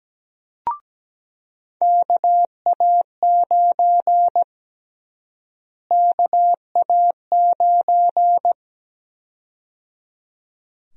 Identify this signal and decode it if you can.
Morse code, heard as a keyed tone, about 17 wpm, 700 Hz. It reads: KA9 KA9
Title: USA Call Sign Prefixes - Repeated 2x - 17wpm